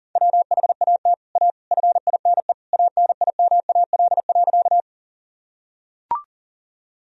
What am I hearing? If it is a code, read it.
Morse code, heard as a keyed tone, about 40 wpm, 700 Hz. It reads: WHAT A FINE ANIMAL.
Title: Sentences from Top 1000 Words - Review Part 1 - 40wpm